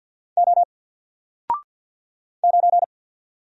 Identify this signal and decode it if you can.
Morse code, heard as a keyed tone, about 48 wpm, 700 Hz. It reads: O 9
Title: Single Letter-Number Alt 50wpm